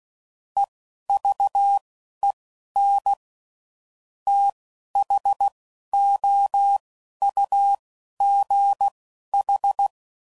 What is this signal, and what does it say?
Morse code, heard as a keyed tone, about 16 wpm, spaced as for 11 wpm, 800 Hz. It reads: EVEN THOUGH
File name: CW Code Groups @10WPM / Technical Subjects / SwPwrSup